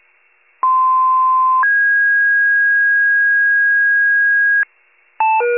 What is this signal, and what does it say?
Two-tone page, heard as a one-second tone, then a three-second tone, about 1040 then 1650 Hz. It is heard from 0.6 to 4.6 s.